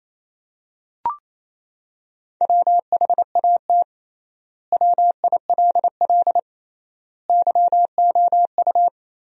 Morse code, heard as a keyed tone, about 28 wpm, 700 Hz. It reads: WHAT WILL YOU
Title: 3x Word-Spacing - Sentences from Top 100 Words - 28wpm